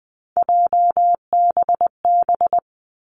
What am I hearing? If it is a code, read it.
Morse code, heard as a keyed tone, about 20 wpm, 700 Hz. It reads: JBB
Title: Three Challenging Letter-Number Combinations 20wpm